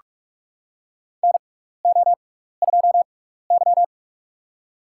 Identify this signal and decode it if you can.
Morse code, heard as a keyed tone, about 42 wpm, spaced as for 15 wpm, 700 Hz. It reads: NO2Y
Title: USA Call Signs - 4 Characters - 6x Character Spacing - 45wpm